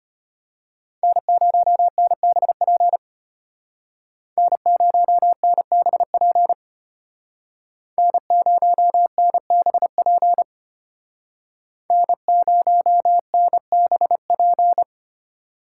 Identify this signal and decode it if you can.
Morse code, heard as a keyed tone, about 32 wpm, 700 Hz. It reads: N0NBP N0NBP N0NBP N0NBP